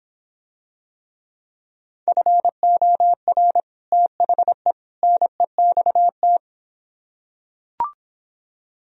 Morse code, heard as a keyed tone, about 26 wpm, 700 Hz. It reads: FOR THE NEXT